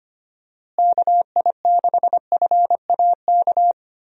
Morse code, heard as a keyed tone, about 25 wpm, 700 Hz. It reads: KI6FAK
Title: USA Call Signs - 6 Characters - Repeated 3x - 25wpm